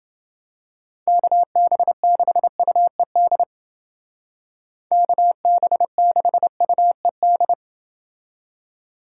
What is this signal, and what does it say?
Morse code, heard as a keyed tone, about 31 wpm, 700 Hz. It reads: KB6UED KB6UED